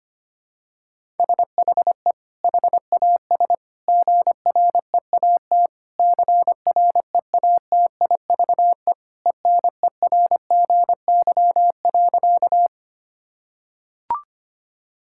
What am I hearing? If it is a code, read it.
Morse code, heard as a keyed tone, about 25 wpm, 700 Hz. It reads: SHE HAS GREAT CREATIVE ENERGY.
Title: Sentences from Top 1000 Words - Review Part 6 - 25wpm